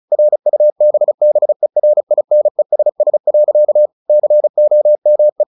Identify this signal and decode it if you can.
Morse code, heard as a keyed tone, about 34 wpm, 600 Hz. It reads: RUBBERINESS. COME
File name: the old man and the sea 2 35wpm